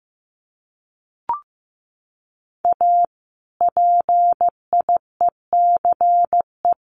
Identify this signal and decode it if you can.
Morse code, heard as a keyed tone, about 15 wpm, 700 Hz. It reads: A PIECE
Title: Sets of 3 Words 15wpm